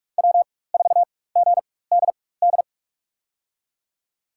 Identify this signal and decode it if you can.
Morse code, heard as a keyed tone, about 46 wpm, spaced as for 21 wpm, 700 Hz. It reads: W4GDD
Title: USA Call Signs - 4x Character Spacing - 45wpm